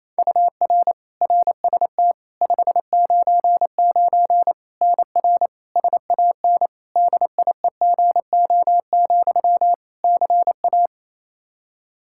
Morse code, heard as a keyed tone, about 28 wpm, 700 Hz. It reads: UR RST 599 NR SAN DIEGO, CA
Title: Practice QSO - Exchange 1 - Beginning - 28wpm